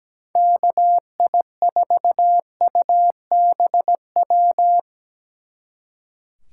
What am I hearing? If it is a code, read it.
Morse code, heard as a keyed tone, about 20 wpm, 700 Hz. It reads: KI4UBW
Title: Speed Racing - USA 2x3 Call Signs 17wpm